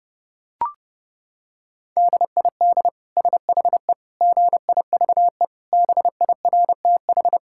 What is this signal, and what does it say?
Morse code, heard as a keyed tone, about 30 wpm, 700 Hz. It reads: DID SHE GIVE BIRTH
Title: Sentences from Top 1000 Words - Review Part 5 - 30wpm